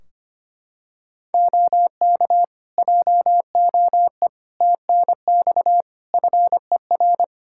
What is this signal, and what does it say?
Morse code, heard as a keyed tone, about 25 wpm, 700 Hz. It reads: OK JOE TNX FER
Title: K1USN SST QSO Practice 25wpm